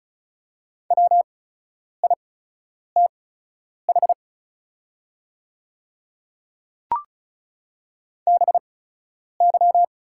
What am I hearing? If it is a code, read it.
Morse code, heard as a keyed tone, about 36 wpm, spaced as for 10 wpm, 700 Hz. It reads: WITH BY
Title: 8x Character Spacing - Top 200 Words - 35wpm